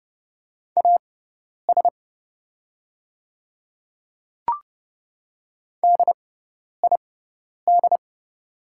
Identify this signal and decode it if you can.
Morse code, heard as a keyed tone, about 30 wpm, spaced as for 10 wpm, 700 Hz. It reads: AS DID
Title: 6x Character Spacing - Top 100 Words - 30wpm